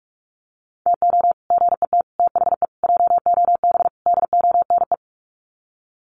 Morse code, heard as a keyed tone, about 44 wpm, 700 Hz. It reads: TO GET THE JOB DONE
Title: Sets of 5 Words Encore2 45wpm